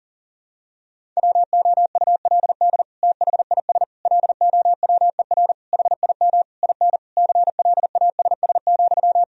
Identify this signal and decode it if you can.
Morse code, heard as a keyed tone, about 40 wpm, 700 Hz. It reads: WOULD THIS LOWER HIM IN CLASS,
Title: Sentences from Top 1000 Words - Review Part 4 - 40wpm